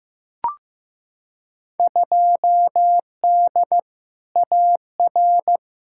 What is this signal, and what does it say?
Morse code, heard as a keyed tone, about 15 wpm, 700 Hz. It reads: 2D AR
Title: Field Day Exchanges Encore - 15wpm